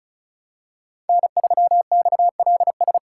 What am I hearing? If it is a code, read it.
Morse code, heard as a keyed tone, about 36 wpm, 700 Hz. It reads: N3XLS
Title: K1USN SST QSO Practice - 35wpm